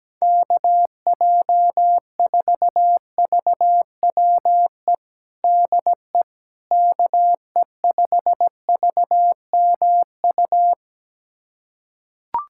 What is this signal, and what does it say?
Morse code, heard as a keyed tone, about 17 wpm, 700 Hz. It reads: KJ4VWE DE KE5VMU
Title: USA Callsign DE Callsign - 6 Characters - 17wpm